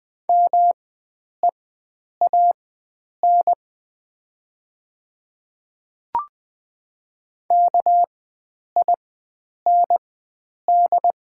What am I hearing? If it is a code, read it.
Morse code, heard as a keyed tone, about 20 wpm, spaced as for 9 wpm, 690 Hz. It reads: MEAN KIND
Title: 4x Character Spacing - Top 300 Words - 20wpm